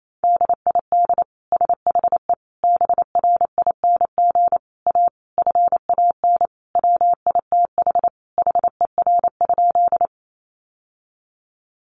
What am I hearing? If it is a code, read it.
Morse code, heard as a keyed tone, about 28 wpm, 700 Hz. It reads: DID SHE BRING A FAN WITH HER?
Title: Sentences from Top 1300 Words 28wpm